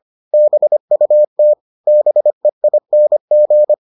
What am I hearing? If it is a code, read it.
Morse code, heard as a keyed tone, about 25 wpm, 600 Hz. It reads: BUT BEING